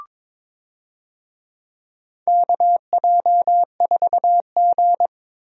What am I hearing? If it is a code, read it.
Morse code, heard as a keyed tone, about 22 wpm, 700 Hz. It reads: KJ4G
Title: USA Call Signs - 4 Characters - 22wpm